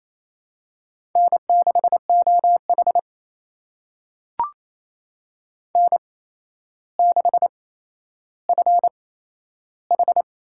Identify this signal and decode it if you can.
Morse code, heard as a keyed tone, about 28 wpm, spaced as for 12 wpm, 700 Hz. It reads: N6OH N6FH
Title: USA Call Signs - 8x Character Spacing - 28wpm